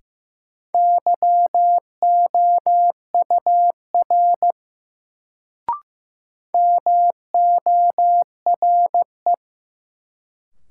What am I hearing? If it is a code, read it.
Morse code, heard as a keyed tone, about 15 wpm, 700 Hz. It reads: YOUR MORE
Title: Sentences from Top 100 Words - New Word Drill - 15wpm